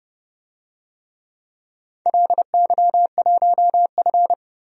Morse code, heard as a keyed tone, about 30 wpm, 700 Hz. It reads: LY1F
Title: DXCC-100 Call Signs - 4 Characters - 30wpm